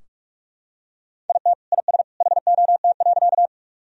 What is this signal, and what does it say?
Morse code, heard as a keyed tone, about 43 wpm, 700 Hz. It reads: IT IS HOT.
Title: Sentences from Top 1000 Words - Review Part 3 - 45wpm